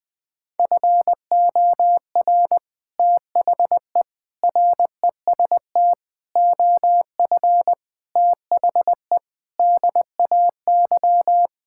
Morse code, heard as a keyed tone, about 20 wpm, 700 Hz. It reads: FOR THE REST OF THE DAY
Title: Sentences with 3-Word Phrases - 20wpm